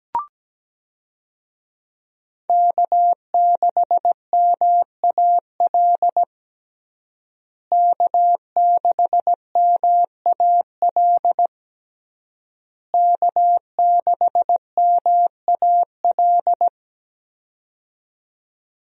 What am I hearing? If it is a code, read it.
Morse code, heard as a keyed tone, about 17 wpm, 700 Hz. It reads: K6MAL K6MAL K6MAL
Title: USA Call Signs - 5 Characters - Repeated 3x - 17wpm